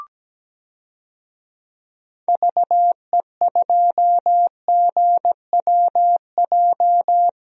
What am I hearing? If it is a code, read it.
Morse code, heard as a keyed tone, about 17 wpm, 700 Hz. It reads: VE2GWJ
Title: DXCC-100 Call Signs - 6 Characters - 17wpm